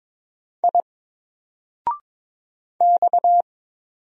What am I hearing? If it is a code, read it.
Morse code, heard as a keyed tone, about 22 wpm, 700 Hz. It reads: I X
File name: Single-Letter-Number-22wpm